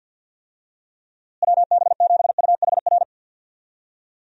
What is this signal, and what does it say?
Morse code, heard as a keyed tone, about 45 wpm, 700 Hz. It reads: WB7UHR
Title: USA Call Signs - 6 Characters - Repeated 3x - 50wpm